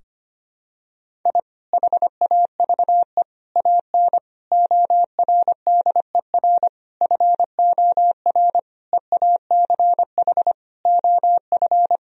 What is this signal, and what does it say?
Morse code, heard as a keyed tone, about 25 wpm, 700 Hz. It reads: I HAVE AN ORDER FOR EACH OF
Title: Sentences from Top 1000 Words - Review Part 2 - 25wpm